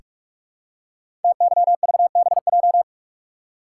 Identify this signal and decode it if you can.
Morse code, heard as a keyed tone, about 44 wpm, 690 Hz. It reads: TYVBJ
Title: Code Groups - 5 Letters - 45wpm